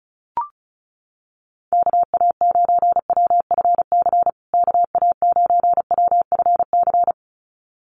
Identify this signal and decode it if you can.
Morse code, heard as a keyed tone, about 34 wpm, 700 Hz. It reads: KA9WFC KA9WFC
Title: USA Call Signs Repeated 2x 35wpm